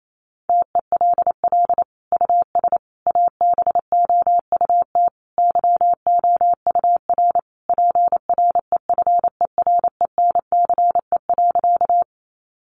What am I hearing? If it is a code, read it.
Morse code, heard as a keyed tone, about 28 wpm, 700 Hz. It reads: TELL US ABOUT YOUR PREFERENCE.
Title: Sentences from Top 1500 Words 28wpm